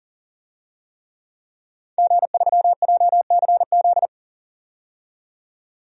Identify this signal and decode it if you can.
Morse code, heard as a keyed tone, about 40 wpm, 700 Hz. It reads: G3JCZ
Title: DXCC-100 Call Signs - 5 Characters - 40wpm